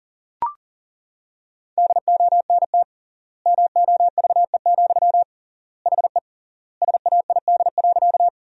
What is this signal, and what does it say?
Morse code, heard as a keyed tone, about 40 wpm, 700 Hz. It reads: DONT MOVE, HE SAID.
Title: 3x Word-Spacing - Sentences from Top 300 Words - 40wpm